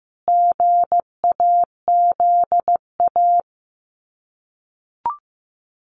Morse code, heard as a keyed tone, about 15 wpm, 700 Hz. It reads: GAZA
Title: Sets of 5 Words 15wpm